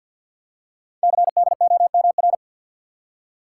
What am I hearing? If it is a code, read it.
Morse code, heard as a keyed tone, about 48 wpm, 700 Hz. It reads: CDOMR
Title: Code Groups - 5 Letters - 50wpm